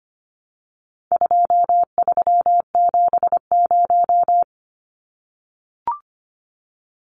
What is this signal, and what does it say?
Morse code, heard as a keyed tone, about 25 wpm, 700 Hz. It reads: 2370
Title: Numbers - 4-Digit 25wpm